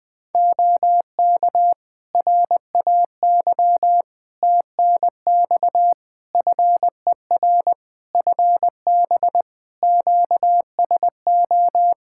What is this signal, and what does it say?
Morse code, heard as a keyed tone, about 20 wpm, 700 Hz. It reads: OK RAY TNX FER FB QSO
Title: Practice QSO - Exchange 4 - Ending - 20wpm